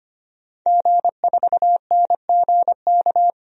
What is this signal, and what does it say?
Morse code, heard as a keyed tone, about 25 wpm, 700 Hz. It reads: G4NGK
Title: DXCC-100 Call Signs - 5 Characters - Repeated 3x - 25wpm